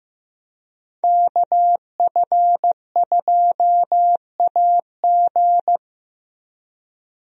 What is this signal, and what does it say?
Morse code, heard as a keyed tone, about 16 wpm, 700 Hz. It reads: KF2AG
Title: USA Call Signs - 5 Characters - Repeated 3x - 15wpm